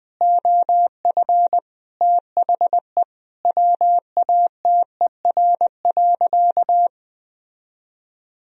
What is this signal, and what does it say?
Morse code, heard as a keyed tone, about 21 wpm, 700 Hz. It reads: OF THE WATER.
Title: Sentences from Top 1000 Words - Review Part 2 - 20wpm